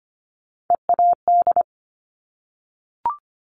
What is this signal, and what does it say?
Morse code, heard as a keyed tone, about 25 wpm, 700 Hz. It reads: EAD
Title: Top Three Letter Combinations in Words 25wpm